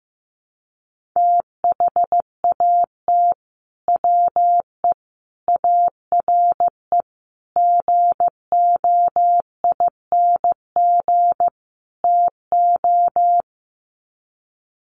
Morse code, heard as a keyed tone, about 15 wpm, 700 Hz. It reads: THAT WE ARE GOING TO
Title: Sets of 5 Words Encore 15wpm